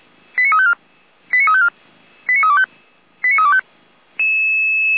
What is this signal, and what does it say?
Five-tone selective call (ZVEI), heard as quick runs of five tones, tones of about 70 ms. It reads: 89354 89354 89325 89325